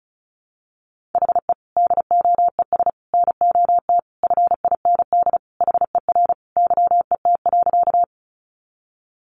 Morse code, heard as a keyed tone, about 34 wpm, 700 Hz. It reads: HE DOES NOT FIND HER YET.
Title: Easy Phrases 35wpm